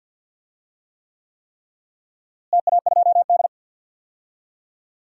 Morse code, heard as a keyed tone, about 46 wpm, 700 Hz. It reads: TA2D
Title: DXCC-100 Call Signs - 4 Characters - Repeated 3x - 50wpm